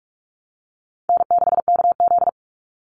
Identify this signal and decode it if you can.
Morse code, heard as a keyed tone, about 44 wpm, 700 Hz. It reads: N6KZ
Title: NA SOTA Chaser Callsigns 45wpm